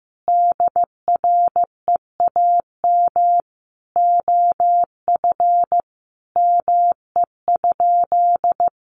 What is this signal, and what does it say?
Morse code, heard as a keyed tone, about 15 wpm, 700 Hz. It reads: DREAM OF ME?
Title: Sentences from Top 1500 Words 15wpm